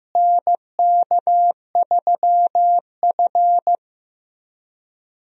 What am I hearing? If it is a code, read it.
Morse code, heard as a keyed tone, about 15 wpm, 700 Hz. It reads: NK3F
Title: USA Call Signs - 4 Characters - Repeated 3x - 15wpm